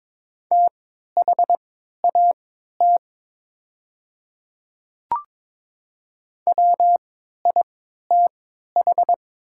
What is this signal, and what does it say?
Morse code, heard as a keyed tone, about 22 wpm, spaced as for 12 wpm, 700 Hz. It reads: THAT WITH
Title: 3x Character Spacing - Top 100 Words - 22wpm